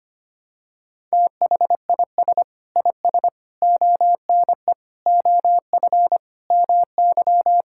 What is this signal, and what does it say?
Morse code, heard as a keyed tone, about 25 wpm, 700 Hz. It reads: THIS IS ONE OF MY
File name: practice / 25wpm / Sentences with 4-Word Phrases - 25wpm